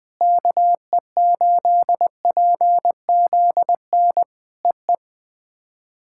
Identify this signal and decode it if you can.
Morse code, heard as a keyed tone, about 19 wpm, 700 Hz. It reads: KE8PZN EE
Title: POTA QSO Practice - 20wpm